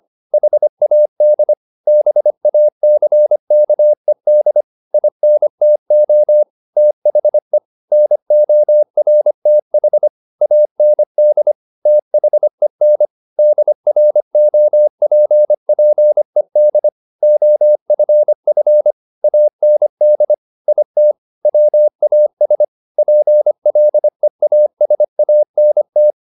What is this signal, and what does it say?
Morse code, heard as a keyed tone, about 25 wpm, 600 Hz. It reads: HAD BACKED INTO THE NORTH AND THEN DROPPED OFF AND IT WAS PLEASANT
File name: the old man and the sea part 1 25wpm